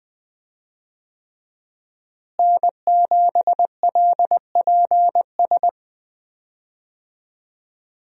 Morse code, heard as a keyed tone, about 20 wpm, 700 Hz. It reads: N7LPS
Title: USA Call Signs - 5 Characters - Repeated 3x - 20wpm